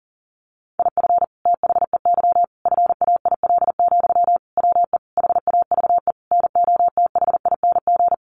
Morse code, heard as a keyed tone, about 40 wpm, 700 Hz. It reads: IF THEY FAIL, WE HAVE NOTHING
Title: Sentences from Top 1100 Words 40wpm